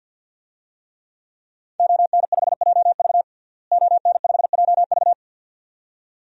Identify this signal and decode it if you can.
Morse code, heard as a keyed tone, about 50 wpm, 700 Hz. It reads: ON5JV ON5JV
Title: DXCC-100 Call Signs - 5 Characters - Repeated 2x - 50wpm